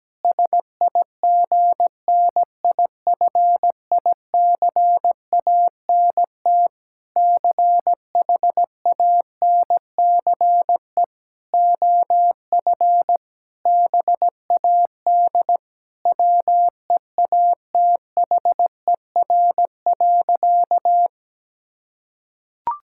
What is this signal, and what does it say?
Morse code, heard as a keyed tone, about 17 wpm, 700 Hz. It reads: SIGNIFICANT CHANCE OF BAD WEATHER.
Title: Sentences from Top 1000 Words - Review Part 6 - 17wpm